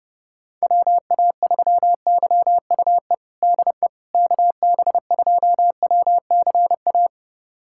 Morse code, heard as a keyed tone, about 30 wpm, 700 Hz. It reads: WA3YUE DE KB2WCA
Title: USA Callsign DE Callsign 30wpm